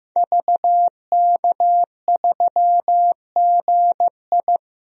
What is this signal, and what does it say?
Morse code, heard as a keyed tone, about 15 wpm, 700 Hz. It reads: VK3GI
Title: DXCC-100 Call Signs - 5 Characters - 15wpm